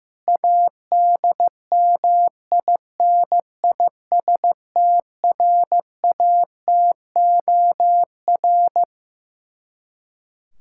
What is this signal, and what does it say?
Morse code, heard as a keyed tone, about 16 wpm, 700 Hz. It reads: ADMINISTRATOR
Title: K1USN SST QSO Practice 15wpm